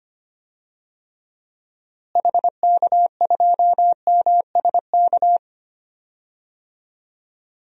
Morse code, heard as a keyed tone, about 25 wpm, 700 Hz. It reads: HK2MSK
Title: DXCC-100 Call Signs - 6 Characters - Repeated 3x - 25wpm